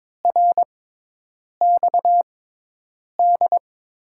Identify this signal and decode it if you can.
Morse code, heard as a keyed tone, about 22 wpm, spaced as for 8 wpm, 700 Hz. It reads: RXD
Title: USA Call Signs - 5 Characters - 6x Character Spacing - 22wpm